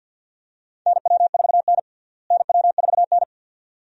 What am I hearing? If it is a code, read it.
Morse code, heard as a keyed tone, about 51 wpm, 700 Hz. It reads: NW4N NW4N
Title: USA Call Signs - 4 Characters - Repeated 2x - 50wpm